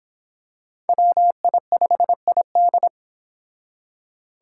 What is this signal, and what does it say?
Morse code, heard as a keyed tone, about 25 wpm, 700 Hz. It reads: WI5ID